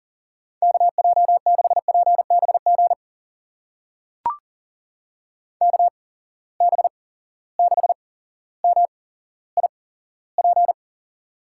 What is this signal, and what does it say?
Morse code, heard as a keyed tone, about 40 wpm, spaced as for 17 wpm, 700 Hz. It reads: KJ6PBG KB6MIP
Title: USA Call Signs - 6 Characters - 8x Character Spacing - 40wpm